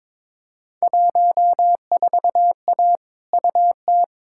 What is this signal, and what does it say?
Morse code, heard as a keyed tone, about 22 wpm, 700 Hz. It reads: 14A UT
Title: Field Day Exchanges Encore - 22wpm